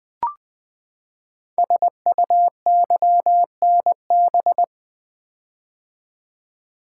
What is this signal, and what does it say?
Morse code, heard as a keyed tone, about 20 wpm, 700 Hz. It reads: SUYNB
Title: Code Groups - 5 Letters - 20wpm